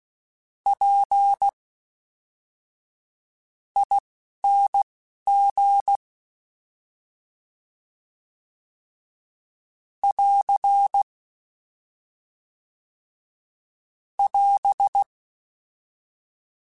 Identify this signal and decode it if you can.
Morse code, heard as a keyed tone, about 16 wpm, spaced as for 12 wpm, 800 Hz. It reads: P ING <AR> <AS>
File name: Misc / Misc-mp3s / Jumps @10